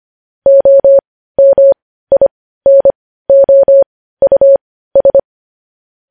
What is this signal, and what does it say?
Morse code, heard as a keyed tone, about 25 wpm, spaced as for 15 wpm, 550 Hz. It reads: OMINOUS
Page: News Headlines in Morse Code at 15 WPM